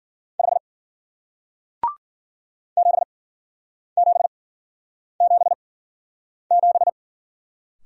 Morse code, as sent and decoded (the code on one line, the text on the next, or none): ..... / --... --... --... --...
5 7777